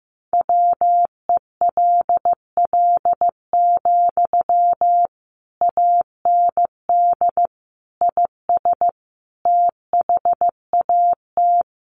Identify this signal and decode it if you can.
Morse code, heard as a keyed tone, about 15 wpm, 700 Hz. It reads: WELL, AND IS THAT
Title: Sentences from Top 200 Words 15wpm